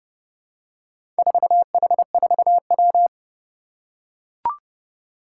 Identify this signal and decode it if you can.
Morse code, heard as a keyed tone, about 30 wpm, 690 Hz. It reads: 4H4W
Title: Four Challenging Letter-Number Combinations 30wpm